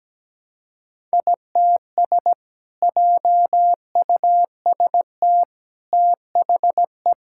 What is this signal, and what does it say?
Morse code, heard as a keyed tone, about 17 wpm, 700 Hz. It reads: ITS JUST THE
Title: Sentences from Top 1000 Words - Review Part 5 - 17wpm